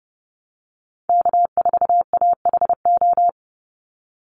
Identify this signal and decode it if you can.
Morse code, heard as a keyed tone, about 30 wpm, 700 Hz. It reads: K4AHO
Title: Call Signs Repeated 2x - 30wpm